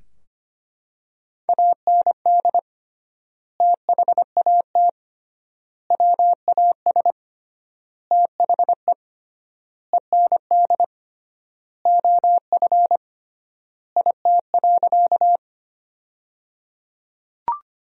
Morse code, heard as a keyed tone, about 25 wpm, 700 Hz. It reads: AND THAT WAS THE END OF IT.
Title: 3x Word-Spacing - Sentences from Top 200 Words - 25wpm